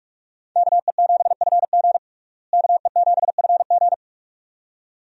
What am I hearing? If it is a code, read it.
Morse code, heard as a keyed tone, about 41 wpm, 700 Hz. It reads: KE7FG KE7FG
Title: USA Call Signs - 5 Characters - Repeated 2x - 45wpm